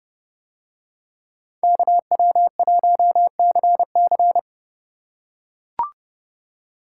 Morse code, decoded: KW1CC

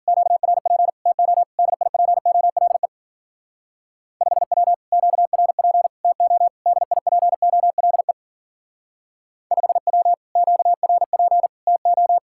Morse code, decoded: QRP TO DIPOLE 5W QRP TO DIPOLE 5W QRP TO

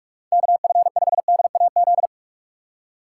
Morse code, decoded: KF5DAZ

